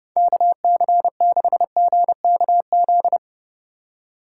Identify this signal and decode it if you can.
Morse code, heard as a keyed tone, about 30 wpm, 700 Hz. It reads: KC6GKZ
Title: USA Call Signs - 4x Character Spacing - 30wpm